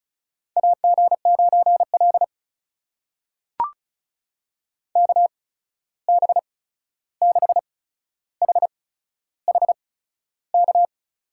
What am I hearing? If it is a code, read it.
Morse code, heard as a keyed tone, about 36 wpm, spaced as for 13 wpm, 700 Hz. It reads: AG9L KB6HHK
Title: USA Call Signs - 8x Character Spacing - 35wpm